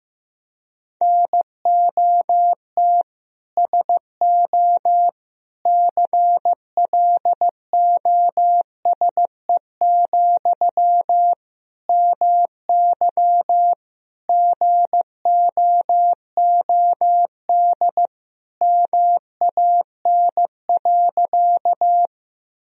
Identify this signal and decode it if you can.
Morse code, heard as a keyed tone, about 15 wpm, 700 Hz. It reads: NOT SO CLOSE, MY GOOD MAN.
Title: Sentences from Top 1000 Words - Review Part 1 - 15wpm